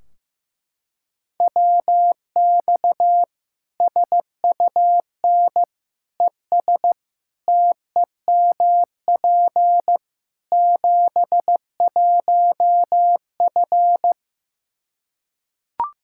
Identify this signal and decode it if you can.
Morse code, heard as a keyed tone, about 16 wpm, 700 Hz. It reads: WX SUN ES TEMP 71F
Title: Practice QSO - Exchange 2 - Info - 15wpm